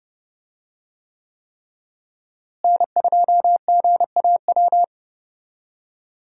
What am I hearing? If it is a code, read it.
Morse code, heard as a keyed tone, about 30 wpm, 700 Hz. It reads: N2GAW
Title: USA Call Signs - 5 Characters - Repeated 3x - 30wpm